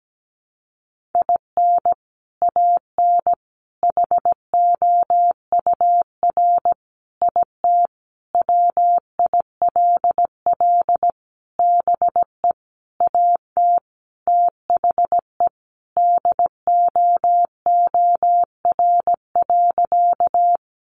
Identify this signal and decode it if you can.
Morse code, heard as a keyed tone, about 17 wpm, 700 Hz. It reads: IN AN HOUR IT WILL BE AT THE DOOR.
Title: Sentences from Top 400 Words 17wpm